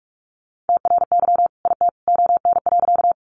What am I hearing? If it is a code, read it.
Morse code, heard as a keyed tone, about 44 wpm, 700 Hz. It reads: TRY IT ON.
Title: Sentences from Top 200 Words 45wpm